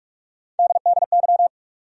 Morse code, heard as a keyed tone, about 46 wpm, 700 Hz. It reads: DDY
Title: USA Call Sign Postfixes - 3 Letters 45wpm